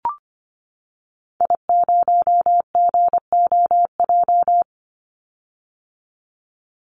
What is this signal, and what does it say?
Morse code, heard as a keyed tone, about 25 wpm, 700 Hz. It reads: I0GOJ
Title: Call Signs - 25wpm